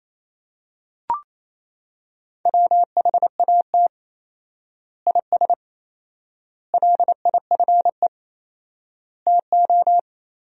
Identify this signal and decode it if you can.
Morse code, heard as a keyed tone, about 28 wpm, 700 Hz. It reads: WHAT IS LIFE TO
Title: 4x Word-Spacing - Sentences from Top 300 Words - 28wpm